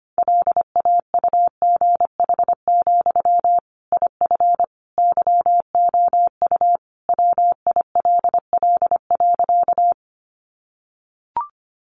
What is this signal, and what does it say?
Morse code, heard as a keyed tone, about 25 wpm, 700 Hz. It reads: LAUGH, IF YOU WILL.
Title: Sentences from Top 500 Words 25wpm